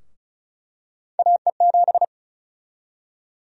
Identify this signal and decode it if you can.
Morse code, heard as a keyed tone, about 34 wpm, 700 Hz. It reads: AE7
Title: USA Call Sign Prefixes - 35wpm